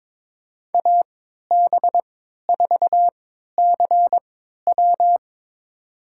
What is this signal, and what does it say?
Morse code, heard as a keyed tone, about 22 wpm, spaced as for 12 wpm, 700 Hz. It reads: AB4CW
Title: USA Call Signs - 5 Characters - 3x Character Spacing - 22wpm